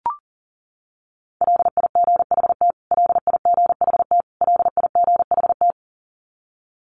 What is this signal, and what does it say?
Morse code, heard as a keyed tone, about 40 wpm, 700 Hz. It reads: LIGHT LIGHT LIGHT
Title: Top 500 Words Repeated 40wpm